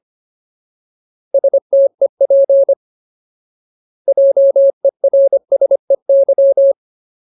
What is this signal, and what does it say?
Morse code, heard as a keyed tone, about 25 wpm, 550 Hz. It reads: STEP JERSEY
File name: Ultimate Word List / 25WPM Ultimate